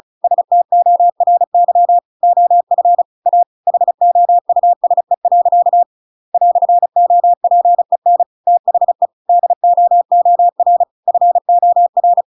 Morse code, heard as a keyed tone, about 34 wpm, 700 Hz. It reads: STORY OF A HOUSE. "OPEN THE DOOR FOR
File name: Pinocchio / Pinocchio ch7 35 35 ik1hge